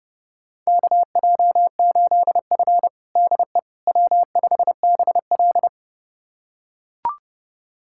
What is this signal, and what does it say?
Morse code, heard as a keyed tone, about 30 wpm, 700 Hz. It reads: KJ8F DE W5BL
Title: USA Callsign DE Callsign - 4 Characters - 30wpm